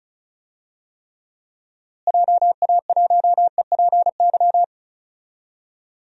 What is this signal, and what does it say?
Morse code, heard as a keyed tone, about 36 wpm, 700 Hz. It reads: JA1EPY
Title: DXCC-100 Call Signs - 6 Characters - Repeated 2x - 35wpm